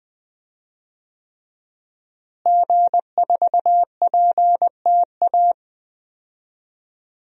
Morse code, heard as a keyed tone, about 20 wpm, 700 Hz. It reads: G4PTA
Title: DXCC-100 Call Signs - 5 Characters - Repeated 3x - 20wpm